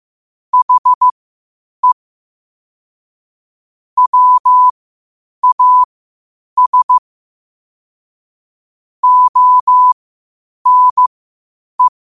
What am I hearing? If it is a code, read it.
Morse code, heard as a keyed tone, about 15 wpm, spaced as for 8 wpm, 1000 Hz. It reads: HE WAS ONE